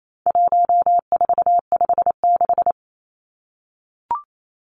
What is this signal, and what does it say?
Morse code, heard as a keyed tone, about 28 wpm, 700 Hz. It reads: 1456
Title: Numbers - 4-Digit 28wpm